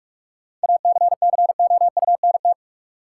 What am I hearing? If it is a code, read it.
Morse code, heard as a keyed tone, about 41 wpm, 700 Hz. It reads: ACCOUNT
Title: Sentences from Top 900 Words - New Word Drill - 45wpm